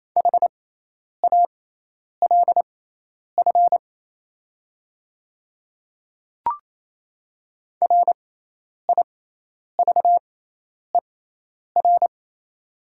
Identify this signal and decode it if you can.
Morse code, heard as a keyed tone, about 28 wpm, spaced as for 10 wpm, 700 Hz. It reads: HALF RIVER
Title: 6x Character Spacing - Top 500 Words - 28wpm